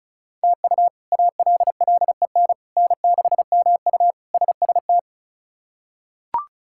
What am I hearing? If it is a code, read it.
Morse code, heard as a keyed tone, about 34 wpm, 700 Hz. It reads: TU ALLEN N6MU SST
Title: K1USN SST QSO Practice - 35wpm